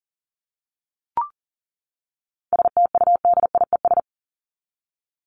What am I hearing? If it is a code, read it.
Morse code, heard as a keyed tone, about 40 wpm, 700 Hz. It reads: STUDIES
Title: ES:S Suffix - Top 100 Words with Base Words 40wpm